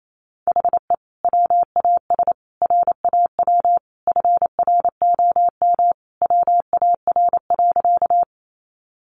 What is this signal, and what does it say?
Morse code, heard as a keyed tone, about 28 wpm, 700 Hz. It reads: HE WAS RAW FROM WAR.
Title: Sentences from Top 1500 Words 28wpm